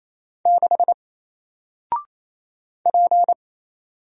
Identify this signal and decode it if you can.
Morse code, heard as a keyed tone, about 28 wpm, 700 Hz. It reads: 6 P
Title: Single Letter-Number 28wpm